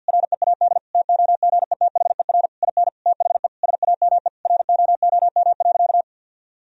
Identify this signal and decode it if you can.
Morse code, heard as a keyed tone, about 50 wpm, 700 Hz. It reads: READ TOGETHER IN THE SAME ROOM.
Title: Sentences from Top 1000 Words - Review Part 3 - 50wpm